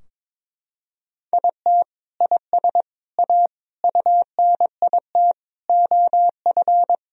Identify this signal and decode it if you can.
Morse code, heard as a keyed tone, about 22 wpm, 700 Hz. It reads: IT IS A UNIT OF
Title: Sentences from Top 1000 Words - Review Part 2 - 22wpm